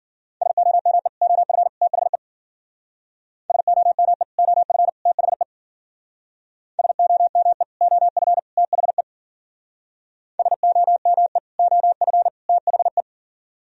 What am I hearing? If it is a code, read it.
Morse code, heard as a keyed tone, about 49 wpm, 700 Hz. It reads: SOME OF THE SOME OF THE SOME OF THE SOME OF THE
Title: Speed-Racing - Sets of 3 Words 40wpm